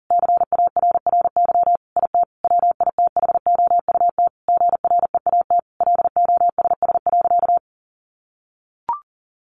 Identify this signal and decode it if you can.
Morse code, heard as a keyed tone, about 40 wpm, 700 Hz. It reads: CARRY IT WITHOUT GREAT LOSS.
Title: Sentences from Top 1400 Words 40wpm